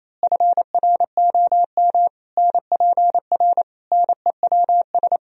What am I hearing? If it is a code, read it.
Morse code, heard as a keyed tone, about 27 wpm, 700 Hz. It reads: FROM NPR NEWS
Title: Speed-Racing - Sets of 5 Words 28wpm